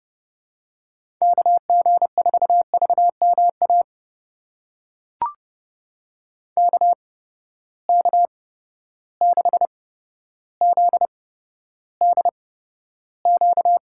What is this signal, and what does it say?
Morse code, heard as a keyed tone, about 30 wpm, spaced as for 13 wpm, 700 Hz. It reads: KG4VMA KK6ZDQ
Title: USA Call Signs - 6 Characters - 8x Character Spacing - 30wpm